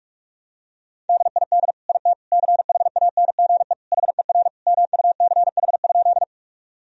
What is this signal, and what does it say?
Morse code, heard as a keyed tone, about 46 wpm, 700 Hz. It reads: DID IT CHANGE HER MUCH?
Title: Sentences from Top 1000 Words - Review Part 1 - 45wpm